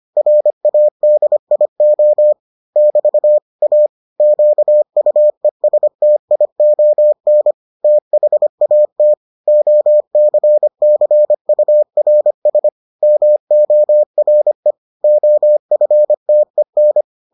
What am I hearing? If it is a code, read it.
Morse code, heard as a keyed tone, about 25 wpm, 600 Hz. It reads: RADIO <BT> A QUESTION THAT OCCURS MORE OFTEN